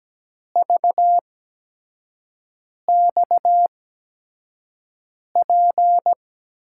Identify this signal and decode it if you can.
Morse code, heard as a keyed tone, about 17 wpm, spaced as for 5 wpm, 700 Hz. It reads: VXP
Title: USA Call Signs - 8x Character Spacing - 17wpm